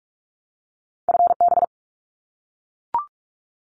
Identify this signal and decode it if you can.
Morse code, heard as a keyed tone, about 44 wpm, 700 Hz. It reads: FB
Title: Abbreviations 45wpm